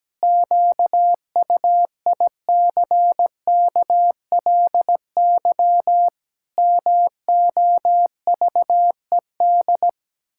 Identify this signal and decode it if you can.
Morse code, heard as a keyed tone, about 17 wpm, 700 Hz. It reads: QUICKLY MOVED
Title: Pangram Challenge v2 - 17wpm